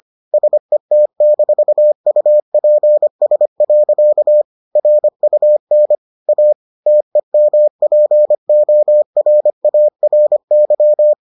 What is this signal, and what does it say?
Morse code, heard as a keyed tone, about 25 wpm, 600 Hz. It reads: SET-UPS. RUN A TEMPORARY